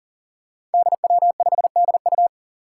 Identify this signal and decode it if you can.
Morse code, heard as a keyed tone, about 39 wpm, 700 Hz. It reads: DW5DU
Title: Five Challenging Letter-Number Combinations 40wpm